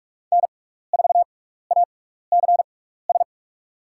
Morse code, heard as a keyed tone, about 42 wpm, spaced as for 15 wpm, 700 Hz. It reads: N4ACS
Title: USA Call Signs - 5 Characters - 6x Character Spacing - 45wpm